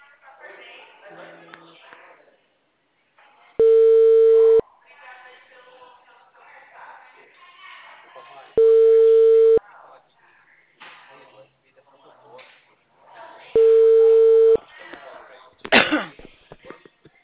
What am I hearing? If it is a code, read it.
Ringback tone (China, 450 Hz), with speech.
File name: out- -2015- - - .1245